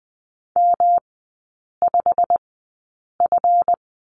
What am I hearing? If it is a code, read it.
Morse code, heard as a keyed tone, about 20 wpm, spaced as for 8 wpm, 690 Hz. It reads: M5F
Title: 191 Lesson 31e - Letter v - Three Characters 20wpm